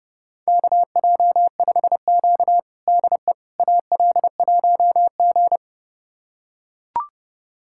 Morse code, heard as a keyed tone, about 32 wpm, 700 Hz. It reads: KJ5Q DE AL1G
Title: USA Callsign DE Callsign - 4 Characters - 30wpm